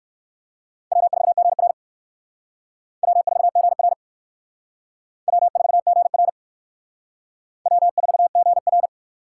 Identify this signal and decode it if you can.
Morse code, heard as a keyed tone, about 56 wpm, 700 Hz. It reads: W4GR W4GR W4GR W4GR